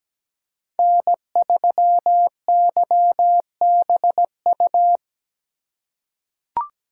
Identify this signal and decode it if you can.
Morse code, heard as a keyed tone, about 17 wpm, 700 Hz. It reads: N3YBU
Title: USA Call Signs - 5 Characters - 4x Character Spacing - 17wpm